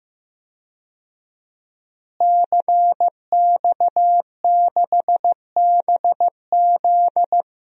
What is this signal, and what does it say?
Morse code, heard as a keyed tone, about 16 wpm, 700 Hz. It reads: CX6BZ
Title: DXCC-100 Call Signs - 5 Characters - Repeated 2x - 15wpm